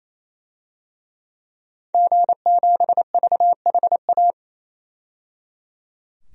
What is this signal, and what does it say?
Morse code, heard as a keyed tone, about 28 wpm, 700 Hz. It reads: G7VHA
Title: DXCC-100 Call Signs - 5 Characters - 28wpm